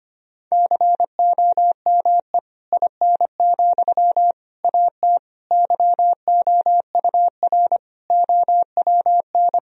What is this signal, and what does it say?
Morse code, heard as a keyed tone, about 25 wpm, 700 Hz. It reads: COME IN, AT YOUR OWN
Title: Sentences from Top 1000 Words - Review Part 4 - 25wpm